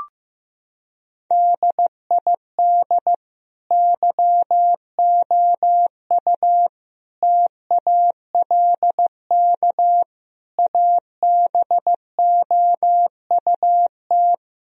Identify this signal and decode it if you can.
Morse code, heard as a keyed tone, about 16 wpm, 700 Hz. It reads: DID YOU TALK ABOUT
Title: Sentences from Top 1000 Words - Review Part 4 - 15wpm